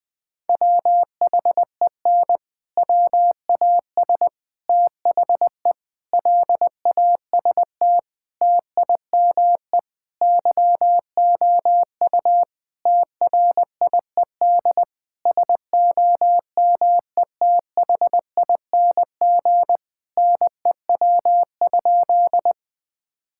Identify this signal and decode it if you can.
Morse code, heard as a keyed tone, about 20 wpm, 700 Hz. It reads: WHEN WAS THE LAST TIME YOU TRIED SOMETHING NEW?
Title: Sentences with 6-Word Phrases - Spoken Beginning - 20wpm